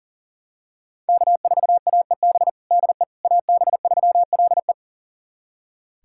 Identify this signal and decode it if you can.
Morse code, heard as a keyed tone, about 37 wpm, 700 Hz. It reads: K4AEB DE AB3LE
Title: USA Callsign DE Callsign - 5 Characters - 40wpm